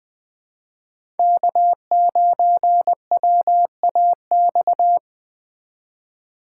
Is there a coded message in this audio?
Yes: Morse code from 1.2 to 5.0 s, heard as a keyed tone, about 19 wpm, 700 Hz. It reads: K9WAX